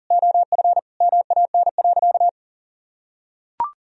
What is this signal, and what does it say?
Morse code, heard as a keyed tone, about 37 wpm, 700 Hz. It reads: OF MAN.
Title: Sentences from Top 1000 Words - Review Part 4 - 40wpm